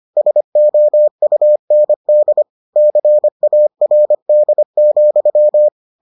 Morse code, heard as a keyed tone, about 25 wpm, 600 Hz. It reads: SOUND CARD,